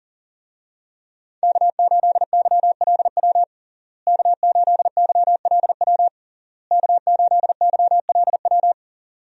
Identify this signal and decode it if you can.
Morse code, heard as a keyed tone, about 37 wpm, 700 Hz. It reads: K8YLW K8YLW K8YLW
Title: USA Call Signs - 5 Characters - Repeated 3x - 40wpm